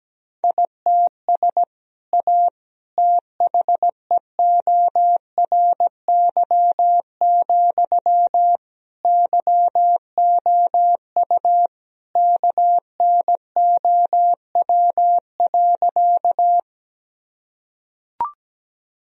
Morse code, heard as a keyed tone, about 17 wpm, 700 Hz. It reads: ITS A THEORY, YOU KNOW.
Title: Sentences from Top 1000 Words - Review Part 3 - 17wpm